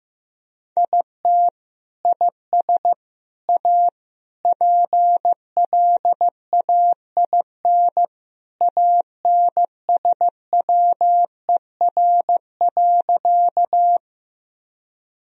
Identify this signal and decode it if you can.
Morse code, heard as a keyed tone, about 15 wpm, 700 Hz. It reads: IT IS A PLAIN ANSWER.
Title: Sentences from Top 1000 Words - Review Part 2 - 15wpm